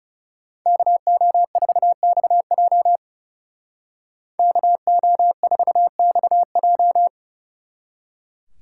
Morse code, heard as a keyed tone, about 35 wpm, 700 Hz. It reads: KO4XJ KO4XJ
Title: Speed Racing - USA 2x2 Call Signs 30wpm